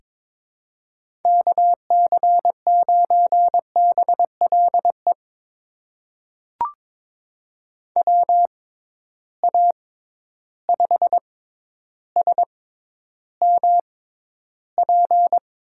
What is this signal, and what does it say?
Morse code, heard as a keyed tone, about 22 wpm, spaced as for 11 wpm, 700 Hz. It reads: KC9BLE WA5SMP
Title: USA Call Signs - 6 Characters - 6x Character Spacing - 22wpm